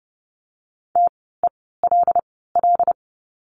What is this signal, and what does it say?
Morse code, heard as a keyed tone, about 30 wpm, spaced as for 17 wpm, 700 Hz. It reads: TELL
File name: course / 30:10wpm / 108 Lesson 18f - Letter m - Words 30wpm